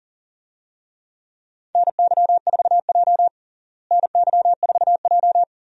Morse code, heard as a keyed tone, about 40 wpm, 700 Hz. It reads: NY4J NY4J
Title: USA Call Signs - 4 Characters - Repeated 3x - 40wpm